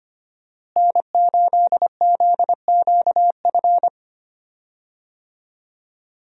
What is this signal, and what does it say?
Morse code, heard as a keyed tone, about 25 wpm, 700 Hz. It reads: N8ZQF
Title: USA Call Signs - 5 Characters - Repeated 3x - 25wpm